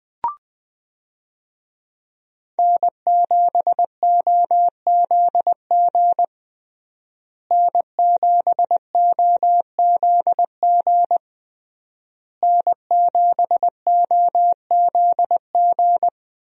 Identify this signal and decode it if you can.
Morse code, heard as a keyed tone, about 20 wpm, 700 Hz. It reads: N7OZG N7OZG N7OZG
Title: USA Call Signs - 5 Characters - Repeated 3x - 20wpm